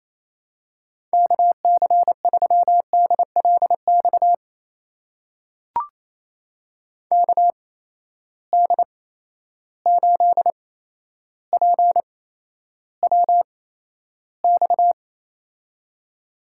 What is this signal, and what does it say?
Morse code, heard as a keyed tone, about 28 wpm, spaced as for 12 wpm, 700 Hz. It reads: KC3DLX KD8PWX
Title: USA Call Signs - 6 Characters - 8x Character Spacing - 28wpm